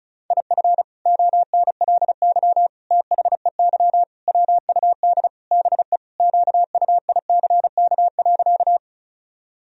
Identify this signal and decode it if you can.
Morse code, heard as a keyed tone, about 36 wpm, 700 Hz. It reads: IF ONLY THEY WUD BE QUICK.
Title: Sentences from Top 500 Words and Abbreviations 35wpm